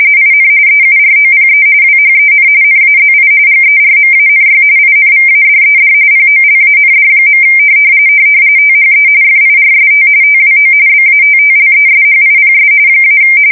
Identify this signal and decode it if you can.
RTTY, about 45.45 baud. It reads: IN EUROPE WHEN NAZI GERMANY INVADED POLAND. THE FIRST TWO BATTLESHIPS -
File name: fsk170 / battleshipnj